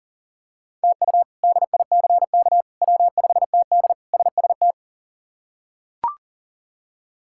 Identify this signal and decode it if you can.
Morse code, heard as a keyed tone, about 40 wpm, 700 Hz. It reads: TU DICK W5TD SST
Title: K1USN SST QSO Practice - 40wpm